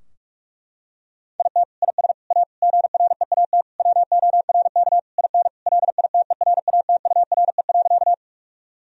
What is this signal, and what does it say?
Morse code, heard as a keyed tone, about 46 wpm, 700 Hz. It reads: IT IS A GREAT WORK IN LITERATURE.
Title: Sentences from Top 1000 Words - Review Part 3 - 45wpm